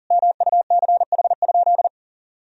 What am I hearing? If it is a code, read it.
Morse code, heard as a keyed tone, about 40 wpm, 700 Hz. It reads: MUCH?
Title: Sentences from Top 1000 Words - Review Part 2 - 40wpm